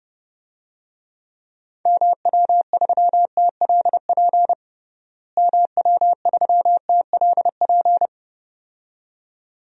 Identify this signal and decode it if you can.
Morse code, heard as a keyed tone, about 32 wpm, 700 Hz. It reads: MW3TLP MW3TLP
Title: DXCC-100 Call Signs - 6 Characters - Repeated 2x - 30wpm